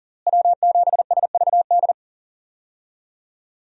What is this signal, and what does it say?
Morse code, heard as a keyed tone, about 39 wpm, 700 Hz. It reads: W7SVD
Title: USA Call Signs - 5 Characters - Repeated 2x - 40wpm